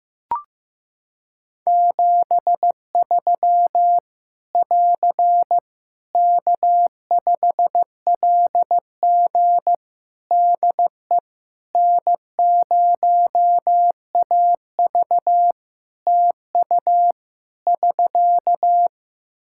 Morse code, heard as a keyed tone, about 15 wpm, 700 Hz. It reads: 73 <AR> K5LG DE N0AV TU <SK>
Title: K1USN SST QSO Practice 15wpm